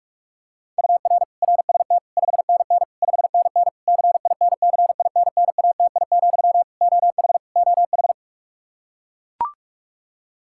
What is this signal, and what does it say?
Morse code, heard as a keyed tone, about 41 wpm, 700 Hz. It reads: UR RST 5NN 5NN CINCINNATI, OH OH
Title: Practice QSO - Exchange 1 - Beginning - 45wpm